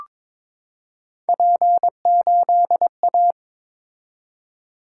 Morse code, heard as a keyed tone, about 23 wpm, 700 Hz. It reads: P8A
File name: Three Letter-Number Combinations - 22wpm